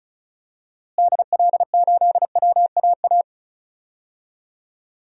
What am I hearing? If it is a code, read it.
Morse code, heard as a keyed tone, about 34 wpm, 700 Hz. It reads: DL8WAA
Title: DXCC-100 Call Signs - 6 Characters - Repeated 2x - 35wpm